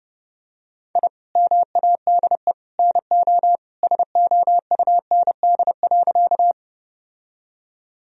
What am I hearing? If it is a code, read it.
Morse code, heard as a keyed tone, about 30 wpm, 700 Hz. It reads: I MADE NO SOUND.
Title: Sentences from Top 1000 Words - Review Part 1 - 30wpm